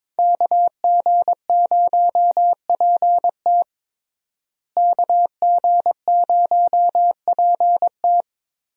Morse code, heard as a keyed tone, about 22 wpm, 700 Hz. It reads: KG0PT KG0PT
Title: USA Call Signs - 5 Characters - Repeated 3x - 22wpm